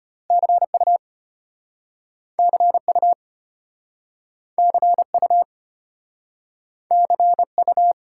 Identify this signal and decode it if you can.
Morse code, heard as a keyed tone, about 35 wpm, 700 Hz. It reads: CU CU CU CU